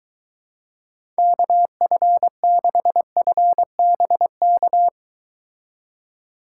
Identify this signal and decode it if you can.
Morse code, heard as a keyed tone, about 23 wpm, 700 Hz. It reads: KF6FBK